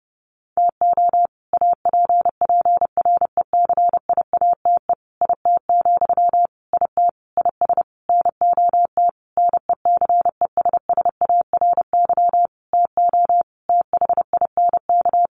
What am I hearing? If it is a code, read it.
Morse code, heard as a keyed tone, about 30 wpm, 700 Hz. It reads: TO APPRECIATE IT, IT IS NOT NECESSARY TO THINK
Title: Sentences from Top 1300 Words 30wpm